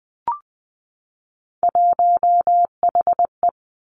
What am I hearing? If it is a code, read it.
Morse code, heard as a keyed tone, about 20 wpm, 700 Hz. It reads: 1HE
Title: Three Challenging Letter-Number Combinations 20wpm